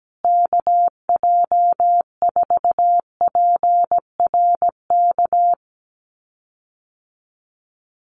Morse code, heard as a keyed tone, about 17 wpm, 700 Hz. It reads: KJ4PRK
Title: USA Call Signs Repeated 3x 17wpm